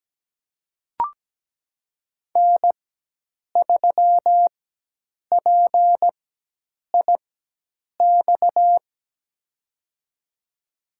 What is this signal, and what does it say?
Morse code, heard as a keyed tone, about 17 wpm, spaced as for 8 wpm, 700 Hz. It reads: N3PIX
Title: USA Call Signs - 5 Characters - 4x Character Spacing - 17wpm